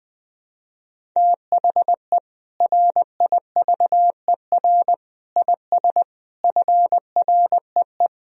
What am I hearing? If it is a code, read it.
Morse code, heard as a keyed tone, about 20 wpm, 700 Hz. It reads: THE RIVER IS FREE
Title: Sentences from Top 1000 Words - Review Part 1 - 20wpm